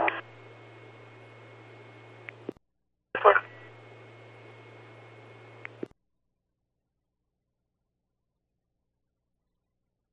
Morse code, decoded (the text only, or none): none